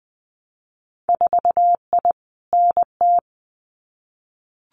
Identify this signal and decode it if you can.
Morse code, heard as a keyed tone, about 20 wpm, 700 Hz. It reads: 4I NT
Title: Winter Field Day Exchanges Encore 20wpm